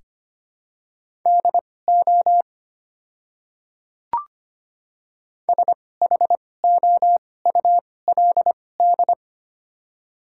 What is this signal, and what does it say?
Morse code, heard as a keyed tone, about 25 wpm, spaced as for 18 wpm, 700 Hz. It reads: DO SHOULD
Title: 2x Character Spacing - Top 500 Words - 25wpm